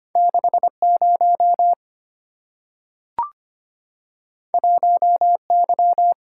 Morse code, heard as a keyed tone, about 25 wpm, 700 Hz. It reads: 6 0 1 Y
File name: Two Letter-Number Combinations - 25wpm